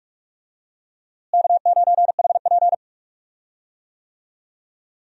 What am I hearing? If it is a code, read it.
Morse code, heard as a keyed tone, about 46 wpm, 700 Hz. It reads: K9HP
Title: USA Call Signs - 4 Characters - Repeated 2x - 45wpm